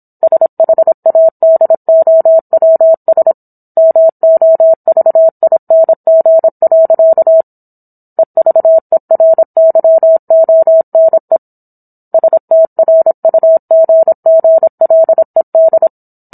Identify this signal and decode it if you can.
Morse code, heard as a keyed tone, about 27 wpm, 650 Hz. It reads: SHADOWS MOVING. EVERYONE STRUGGLED
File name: Book One / 17-Chapter 17 - 26 WPM